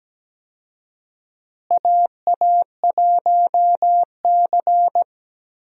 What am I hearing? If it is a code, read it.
Morse code, heard as a keyed tone, about 17 wpm, 700 Hz. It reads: AA1C
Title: USA Call Signs - 4 Characters - Repeated 2x - 17wpm